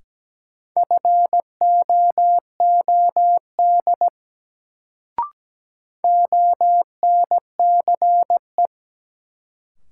Morse code, heard as a keyed tone, about 17 wpm, 700 Hz. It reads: FOOD ONCE
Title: Sentences from Top 300 Words - New Word Drill - 17wpm